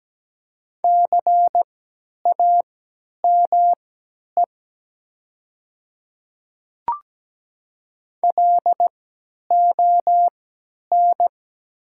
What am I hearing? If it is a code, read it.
Morse code, heard as a keyed tone, about 17 wpm, spaced as for 10 wpm, 700 Hz. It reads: CAME LON
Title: 3x Character Spacing - Top 300 Words - 17wpm